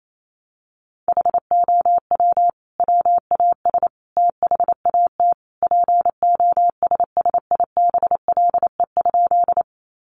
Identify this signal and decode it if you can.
Morse code, heard as a keyed tone, about 28 wpm, 700 Hz. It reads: HOW WAS THAT POSSIBLE?
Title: Sentences from Top 500 Words 28wpm